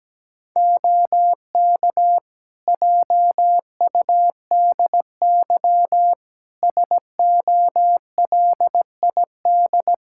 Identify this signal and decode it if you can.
Morse code, heard as a keyed tone, about 17 wpm, 700 Hz. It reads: OK JUDY SOLID
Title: Practice QSO - Exchange 3 - More Info - 17wpm